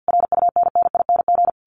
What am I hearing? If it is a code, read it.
Morse code, heard as a keyed tone, about 48 wpm, 700 Hz. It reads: RUNNING
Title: A Sound Of Thunder 50wpm